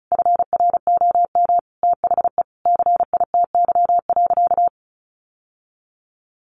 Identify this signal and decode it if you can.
Morse code, heard as a keyed tone, about 34 wpm, 700 Hz. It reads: FROM THE CITY.
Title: Sentences from Top 1500 Words 35wpm